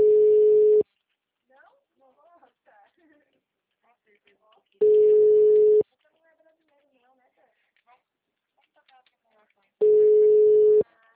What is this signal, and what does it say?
Ringback tone (Europe, 425 Hz), with speech.